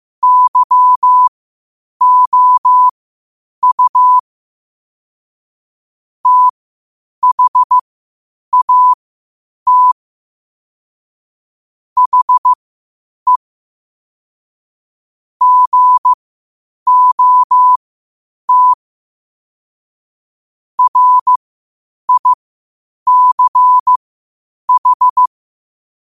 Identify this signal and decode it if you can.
Morse code, heard as a keyed tone, about 15 wpm, spaced as for 8 wpm, 1000 Hz. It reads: YOU THAT HE GOT RICH
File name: qotdfiles / 8wpm